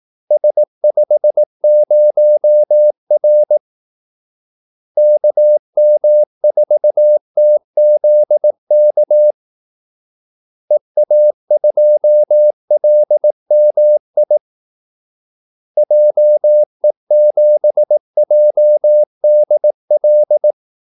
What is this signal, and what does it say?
Morse code, heard as a keyed tone, about 18 wpm, 600 Hz. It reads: S50R KM4TZK EA2LMI JE7JDL
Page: Kaldesignaler 18 wpm